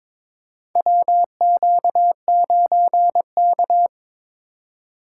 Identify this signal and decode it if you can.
Morse code, heard as a keyed tone, about 22 wpm, 700 Hz. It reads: WQ9K
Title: USA Call Signs - 4 Characters - 3x Character Spacing - 22wpm